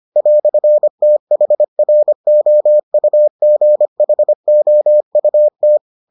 Morse code, heard as a keyed tone, about 25 wpm, 600 Hz. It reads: "THROUGHOUT